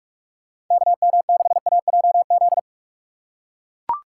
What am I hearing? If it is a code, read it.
Morse code, heard as a keyed tone, about 41 wpm, 690 Hz. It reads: KM6AJZ
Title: USA Call Signs - 6 Characters - 3x Character Spacing - 45wpm